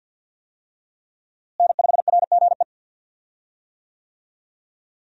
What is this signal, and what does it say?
Morse code, heard as a keyed tone, about 51 wpm, 700 Hz. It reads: N5RGE
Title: USA Call Signs - 5 Characters - Repeated 2x - 50wpm